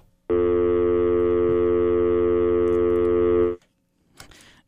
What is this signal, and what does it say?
Dial tone (North America), 350 + 440 Hz.